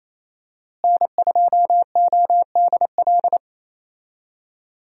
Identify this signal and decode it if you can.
Morse code, heard as a keyed tone, about 28 wpm, 700 Hz. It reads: N2ODL
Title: USA Call Signs - 5 Characters - Repeated 2x - 28wpm